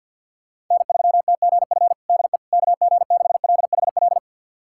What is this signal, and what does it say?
Morse code, heard as a keyed tone, about 51 wpm, 700 Hz. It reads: N3TGF DE KG6LHL
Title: USA Callsign DE Callsign 50wpm